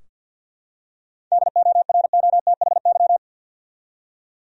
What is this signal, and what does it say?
Morse code, heard as a keyed tone, about 45 wpm, 700 Hz. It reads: DOROTHY
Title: USA Common Names - 50wpm